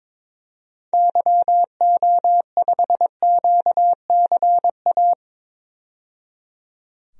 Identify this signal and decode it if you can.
Morse code, heard as a keyed tone, about 22 wpm, 700 Hz. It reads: YO5QCA
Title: DXCC-100 Call Signs - 6 Characters - 22wpm